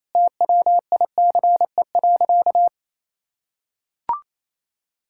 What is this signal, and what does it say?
Morse code, heard as a keyed tone, about 28 wpm, 700 Hz. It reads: TWICE.
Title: Sentences from Top 1600 Words 28wpm